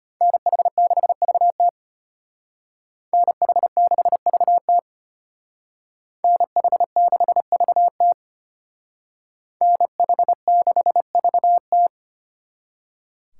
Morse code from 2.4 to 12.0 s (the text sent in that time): NH6VT NH6VT NH6VT